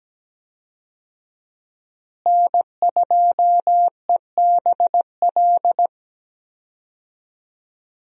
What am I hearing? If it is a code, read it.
Morse code, heard as a keyed tone, about 17 wpm, 700 Hz. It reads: N2EBL
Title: USA Call Signs - 5 Characters - Repeated 3x - 17wpm